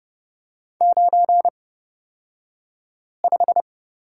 Morse code, heard as a keyed tone, about 30 wpm, spaced as for 5 wpm, 700 Hz. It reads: 95
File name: Lesson 38d - Letter q - Two Characters 30wpm